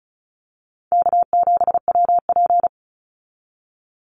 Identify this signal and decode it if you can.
Morse code, heard as a keyed tone, about 34 wpm, 700 Hz. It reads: K7WP
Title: Easy Call Signs Repeated 2x - 35wpm